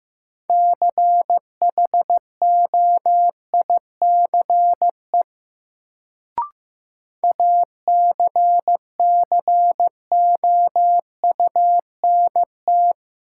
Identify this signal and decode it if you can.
Morse code, heard as a keyed tone, about 15 wpm, 700 Hz. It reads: CHOICE ACCOUNT
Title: Sentences from Top 900 Words - New Word Drill - 15wpm